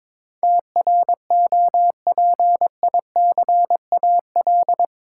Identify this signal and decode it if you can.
Morse code, heard as a keyed tone, about 22 wpm, 700 Hz. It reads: TROPICAL
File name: practice / 22wpm / Eight Letter Words - 22wpm